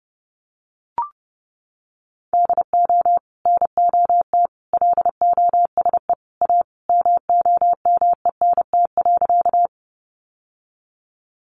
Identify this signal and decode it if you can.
Morse code, heard as a keyed tone, about 30 wpm, 700 Hz. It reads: DO NOT LOSE A MOMENT.
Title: Sentences from Top 1100 Words 30wpm